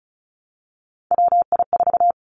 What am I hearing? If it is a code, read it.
Morse code, heard as a keyed tone, about 36 wpm, 700 Hz. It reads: WI4
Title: Three Challenging Letter-Number Combinations 35wpm